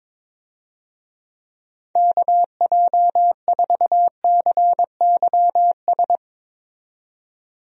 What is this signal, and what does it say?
Morse code, heard as a keyed tone, about 21 wpm, 700 Hz. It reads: KJ4CYS
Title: USA Call Signs - 6 Characters - 22wpm